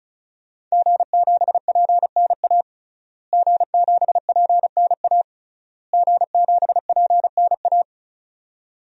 Morse code, heard as a keyed tone, about 36 wpm, 700 Hz. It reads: G7PNA G7PNA G7PNA
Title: DXCC-100 Call Signs - 5 Characters - Repeated 3x - 35wpm